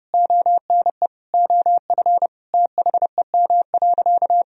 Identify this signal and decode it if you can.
Morse code, heard as a keyed tone, about 30 wpm, 700 Hz. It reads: ONE OF THEM.
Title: Sentences from Top 300 Words and Abbreviations 30wpm